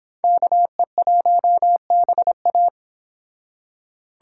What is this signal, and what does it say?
Morse code, heard as a keyed tone, about 29 wpm, 700 Hz. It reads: KE1BA